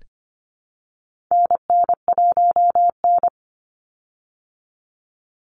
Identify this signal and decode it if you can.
Morse code, heard as a keyed tone, about 25 wpm, 700 Hz. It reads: NN1N
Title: Call Signs Repeated 3x - 25wpm